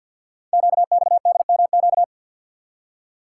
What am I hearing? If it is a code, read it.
Morse code, heard as a keyed tone, about 48 wpm, 690 Hz. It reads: QXDMQ
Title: Code Groups - 5 Letters - 50wpm